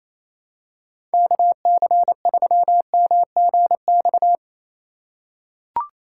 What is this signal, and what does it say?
Morse code, heard as a keyed tone, about 27 wpm, 700 Hz. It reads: KC3MGX
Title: USA Call Signs - 4x Character Spacing - 28wpm